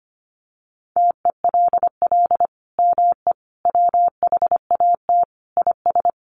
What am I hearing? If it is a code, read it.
Morse code, heard as a keyed tone, about 25 wpm, 700 Hz. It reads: TELL ME WHAT IS
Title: Sentences from Top 700 Words 25wpm